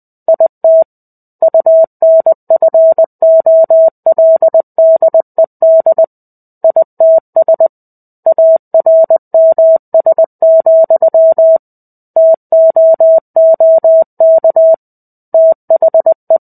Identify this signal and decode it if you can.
Morse code, heard as a keyed tone, about 20 wpm, 650 Hz. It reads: IT UNFOLDED ITS ARMS, TOOK THE
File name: 05-Chapter-05 20 WPM